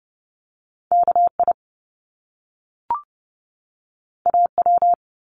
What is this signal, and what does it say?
Morse code, heard as a keyed tone, about 30 wpm, 700 Hz. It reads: KI AW
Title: Top Two Letter Combinations in Words 30wpm